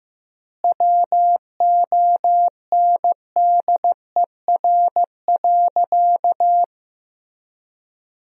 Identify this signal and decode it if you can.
Morse code, heard as a keyed tone, about 14 wpm, 700 Hz. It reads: WONDER.
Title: 4x Word-Spacing - Sentences from Top 500 Words - 15wpm